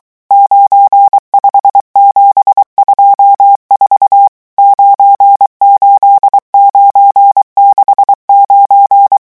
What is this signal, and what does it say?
Morse code, heard as a keyed tone, about 23 wpm, 800 Hz. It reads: 95724 98969
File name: morse mp3 / revision11 chiffres vitesse 22